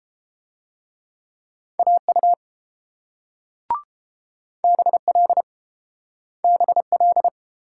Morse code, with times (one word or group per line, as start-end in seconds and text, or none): 1.8-2.3 AU
4.6-5.4 BL
6.4-7.3 BL